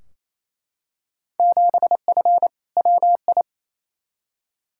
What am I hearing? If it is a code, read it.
Morse code, heard as a keyed tone, about 28 wpm, 700 Hz. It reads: 7F WI
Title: Field Day Exchanges Encore - 28wpm